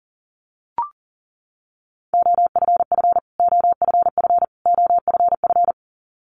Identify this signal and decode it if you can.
Morse code, heard as a keyed tone, about 40 wpm, 700 Hz. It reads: OFF OFF OFF
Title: Top 500 Words Repeated 40wpm